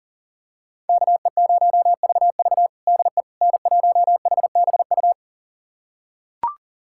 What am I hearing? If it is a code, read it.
Morse code, heard as a keyed tone, about 40 wpm, 700 Hz. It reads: KE0VV DE N1HBU
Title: USA Callsign DE Callsign - 5 Characters - 40wpm